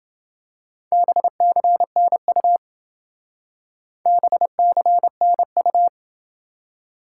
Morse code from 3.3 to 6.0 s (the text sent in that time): BCNU